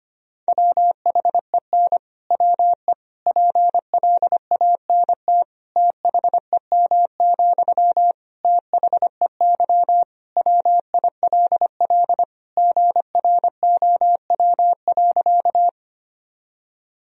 Morse code, heard as a keyed tone, about 25 wpm, 700 Hz. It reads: WHEN WE PLANT THEM, THEY WILL GROW.
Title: Sentences from Top 1000 Words - Review Part 1 - 25wpm